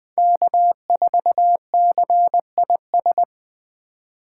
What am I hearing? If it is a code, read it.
Morse code, heard as a keyed tone, about 20 wpm, 700 Hz. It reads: K4CIS
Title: USA Call Signs - 5 Characters - 2x Character Spacing - 20wpm